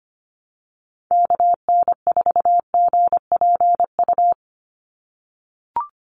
Morse code, heard as a keyed tone, about 25 wpm, 700 Hz. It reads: KN4GPU
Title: USA Call Signs 25wpm